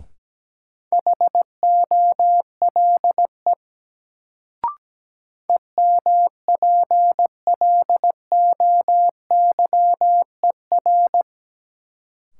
Morse code, heard as a keyed tone, about 17 wpm, 700 Hz. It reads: HOLE EMPLOYER
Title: Sentences from Top 1600 Words - New Word Drill - 17wpm